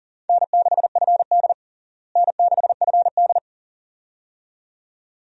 Morse code, heard as a keyed tone, about 40 wpm, 700 Hz. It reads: N6FD N6FD
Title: USA Call Signs - 4 Characters - Repeated 2x - 40wpm